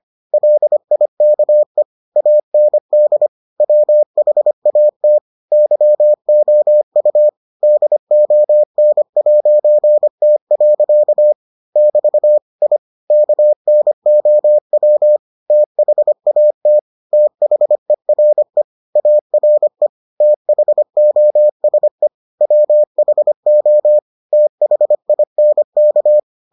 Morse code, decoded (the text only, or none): LIKE AND WHAT YOU DON'T. <BT> I KNOW THAT THERE ARE THOSE WHO THINK